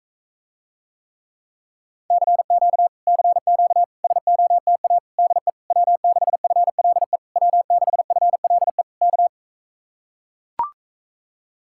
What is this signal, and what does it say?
Morse code, heard as a keyed tone, about 44 wpm, 700 Hz. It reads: CQ CQ SOTA DE W6FLE W6FLE K